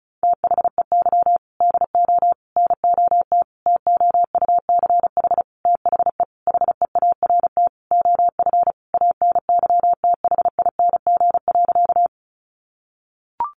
Morse code, heard as a keyed tone, about 34 wpm, 700 Hz. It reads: THEY DO NOT TOUCH THE HEART OF ANYTHING.
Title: Sentences from Top 1200 Words 35wpm